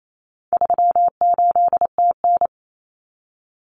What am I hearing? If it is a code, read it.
Morse code, heard as a keyed tone, about 28 wpm, 700 Hz. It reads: 38TN
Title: Common Cut Numbers - 4-Digit 28wpm